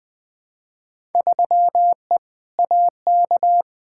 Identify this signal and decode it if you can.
Morse code, heard as a keyed tone, about 21 wpm, 700 Hz. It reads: 3E AK
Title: Field Day Exchanges Encore - 20wpm